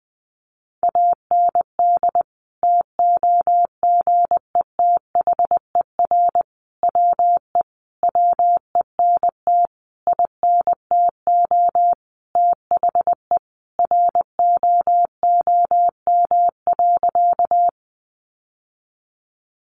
Morse code, heard as a keyed tone, about 20 wpm, 700 Hz. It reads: AND TOGETHER WE WENT INTO THE ROOM.
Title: Sentences from Top 600 Words 20wpm